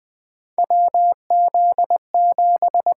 Morse code, heard as a keyed tone, about 21 wpm, 700 Hz. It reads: WZ7
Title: USA Call Signs - 4 Characters - 8x Character Spacing - 20wpm